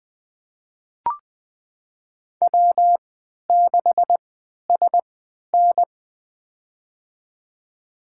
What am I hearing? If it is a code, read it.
Morse code, heard as a keyed tone, about 21 wpm, spaced as for 12 wpm, 700 Hz. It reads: W6SN
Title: USA Call Signs - 4 Characters - 3x Character Spacing - 20wpm